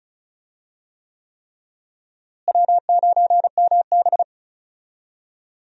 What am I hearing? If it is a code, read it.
Morse code, heard as a keyed tone, about 36 wpm, 700 Hz. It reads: W9MB
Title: USA Call Signs - 4 Characters - Repeated 3x - 35wpm